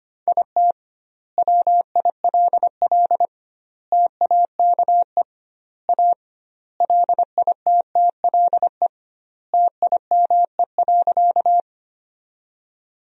Morse code, heard as a keyed tone, about 25 wpm, 700 Hz. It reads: IT WILL TAKE A LITTLE TIME.
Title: 2x Word-Spacing - Sentences from Top 200 Words - 25wpm